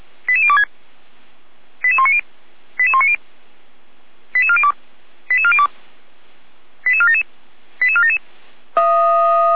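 Five-tone selective call (ZVEI), heard as quick runs of five tones, tones of about 70 ms. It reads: 88026 88190 88190 88402 88402 88477 88477
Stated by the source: ZVEI five-tone selective call received by SDR, Orlamünde fire brigade (feuerwehr-orlamuende).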